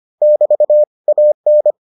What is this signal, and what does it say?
Morse code, heard as a keyed tone, about 25 wpm, 600 Hz. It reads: <BT> AN